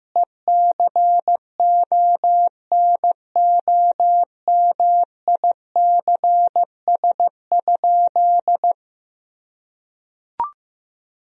Morse code, heard as a keyed tone, about 15 wpm, 700 Hz. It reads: ECONOMICS?
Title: Sentences from Top 1000 Words - Review Part 3 - 15wpm